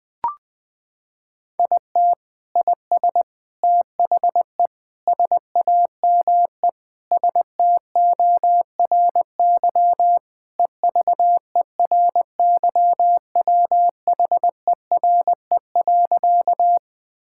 Morse code, heard as a keyed tone, about 20 wpm, 700 Hz. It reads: IT IS THE SAME STORY EVERYWHERE.
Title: Sentences from Top 1600 Words 20wpm